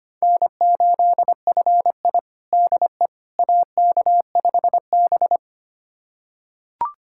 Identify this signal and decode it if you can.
Morse code, heard as a keyed tone, about 25 wpm, 700 Hz. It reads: N8FI DE AK5B
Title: USA Callsign DE Callsign - 4 Characters - 25wpm